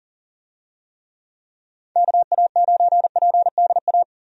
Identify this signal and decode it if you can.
Morse code, heard as a keyed tone, about 40 wpm, 700 Hz. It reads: KA9PDA
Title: USA Call Signs - 6 Characters - 40wpm